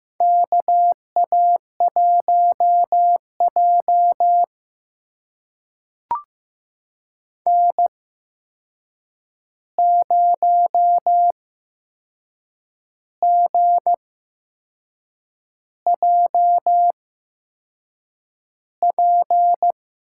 Morse code, heard as a keyed tone, about 15 wpm, spaced as for 6 wpm, 700 Hz. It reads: KA1J N0GJP
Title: USA Call Signs - 8x Character Spacing - 15wpm